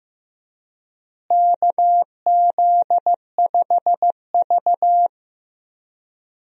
Morse code, heard as a keyed tone, about 15 wpm, 700 Hz. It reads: KZ5V
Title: USA Call Signs - 4 Characters - Repeated 3x - 15wpm